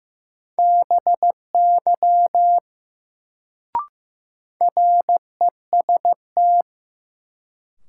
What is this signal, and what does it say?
Morse code, heard as a keyed tone, about 15 wpm, 700 Hz. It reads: BY REST
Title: Sets of 2 Words - New Word Drill - 15wpm